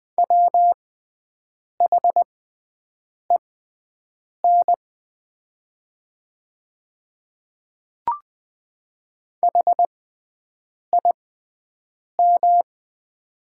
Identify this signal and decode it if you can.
Morse code, heard as a keyed tone, about 21 wpm, spaced as for 7 wpm, 700 Hz. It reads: WHEN HIM
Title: 6x Character Spacing - Top 100 Words - 20wpm